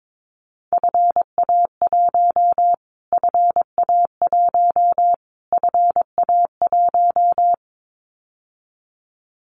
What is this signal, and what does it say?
Morse code, heard as a keyed tone, about 22 wpm, 700 Hz. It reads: FA1 FA1 FA1
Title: Callsign Prefixes for Top 25 DXCC - Repeated 3x - 22wpm